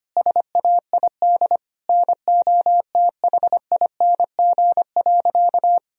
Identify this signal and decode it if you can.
Morse code, heard as a keyed tone, about 25 wpm, 700 Hz. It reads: SAID NOTHING.
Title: Sentences from Top 1000 Words - Review Part 2 - 25wpm